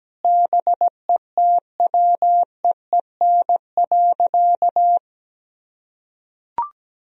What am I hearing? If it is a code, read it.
Morse code, heard as a keyed tone, about 17 wpm, 700 Hz. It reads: BETWEEN.
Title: Sentences from Top 1000 Words - Review Part 1 - 17wpm